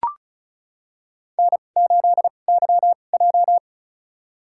Morse code, heard as a keyed tone, about 34 wpm, spaced as for 25 wpm, 700 Hz. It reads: N8YJ
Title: USA Call Signs - 4 Characters - 2x Character Spacing - 35wpm